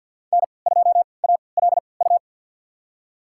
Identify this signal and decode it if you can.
Morse code, heard as a keyed tone, about 53 wpm, spaced as for 29 wpm, 700 Hz. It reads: N2ALU
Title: USA Call Signs - 5 Characters - 3x Character Spacing - 50wpm